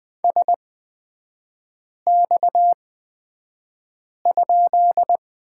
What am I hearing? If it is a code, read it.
Morse code, heard as a keyed tone, about 20 wpm, spaced as for 5 wpm, 700 Hz. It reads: SX?
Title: 243 Lesson 39e - Letter z - Three Characters 20wpm